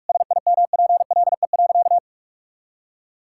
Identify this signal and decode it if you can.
Morse code, heard as a keyed tone, about 46 wpm, 700 Hz. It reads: SIMPLE.
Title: Sentences from Top 1000 Words - Review Part 6 - 45wpm